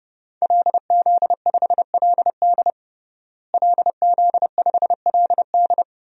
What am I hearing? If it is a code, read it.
Morse code, heard as a keyed tone, about 30 wpm, 700 Hz. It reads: LZ5LD LZ5LD
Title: DXCC-100 Call Signs - 5 Characters - Repeated 2x - 30wpm